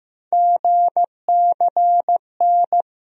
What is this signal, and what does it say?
Morse code, heard as a keyed tone, about 15 wpm, 700 Hz. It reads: GCN
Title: USA Call Sign Postfixes - 3 Letters 15wpm